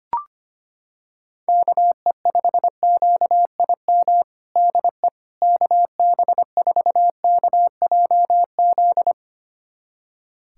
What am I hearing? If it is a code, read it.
Morse code, heard as a keyed tone, about 25 wpm, 700 Hz. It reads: KE5QIM DE KB4KJZ
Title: USA Callsign DE Callsign - 6 Characters - 25wpm